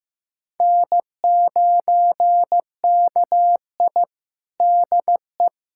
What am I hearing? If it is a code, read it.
Morse code, heard as a keyed tone, about 16 wpm, 700 Hz. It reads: N9KI DE
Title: USA Callsign DE Callsign - 4 Characters - 15wpm